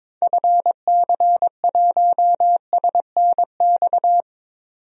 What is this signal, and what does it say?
Morse code, heard as a keyed tone, about 22 wpm, 700 Hz. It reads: FC1SNX
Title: DXCC-100 Call Signs - 6 Characters - 22wpm